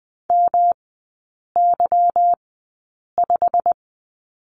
Morse code, heard as a keyed tone, about 20 wpm, spaced as for 8 wpm, 690 Hz. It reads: MY5
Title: 145 Lesson 24e - Letter y - Three Characters 20wpm